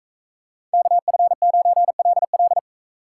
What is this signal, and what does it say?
Morse code, heard as a keyed tone, about 44 wpm, 700 Hz. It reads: KF9LL